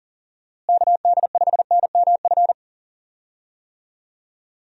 Morse code, heard as a keyed tone, about 40 wpm, 700 Hz. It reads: KD5NMF
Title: USA Call Signs - 6 Characters - Repeated 3x - 40wpm